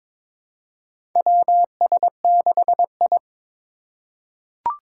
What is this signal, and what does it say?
Morse code, heard as a keyed tone, about 22 wpm, 700 Hz. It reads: WS6I
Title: USA Call Signs - 4 Characters - 4x Character Spacing - 22wpm